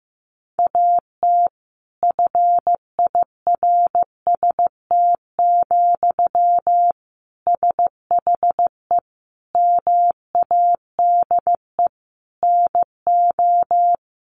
Morse code, heard as a keyed tone, about 15 wpm, 700 Hz. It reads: AT FIRST, SHE MADE NO
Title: Sentences from Top 1400 Words 15wpm